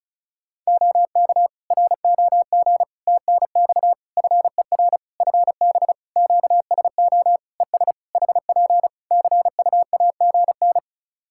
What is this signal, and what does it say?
Morse code, heard as a keyed tone, about 36 wpm, 700 Hz. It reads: OK ROG TNX FER FB QSO ES HP CUAGN
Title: Practice QSO - Exchange 4 - Ending - 35wpm